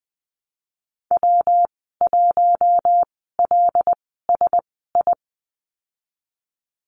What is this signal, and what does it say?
Morse code, heard as a keyed tone, about 20 wpm, spaced as for 15 wpm, 700 Hz. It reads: W1LSI
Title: 122 Lesson 20g - Number 3 - Call Signs 20wpm